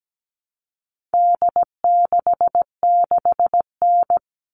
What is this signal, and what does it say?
Morse code, heard as a keyed tone, about 17 wpm, 700 Hz. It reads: D66N
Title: All Cut Numbers - 4-Digit 17wpm